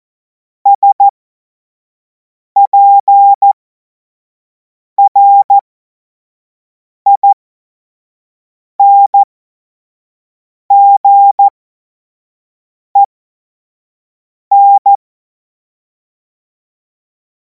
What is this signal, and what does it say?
Morse code, heard as a keyed tone, about 14 wpm, spaced as for 5 wpm, 800 Hz. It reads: SPRINGEN
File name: Morsekurs / Witze / Witz 05. Mai 2026 25 F70